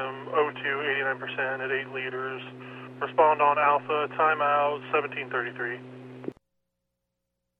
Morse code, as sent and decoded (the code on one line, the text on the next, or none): none